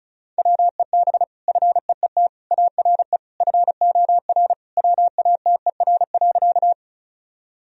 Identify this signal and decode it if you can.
Morse code, heard as a keyed tone, about 36 wpm, 700 Hz. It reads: WEB FEET ARE FOR WATER.
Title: Sentences from Top 1000 Words - Review Part 4 - 35wpm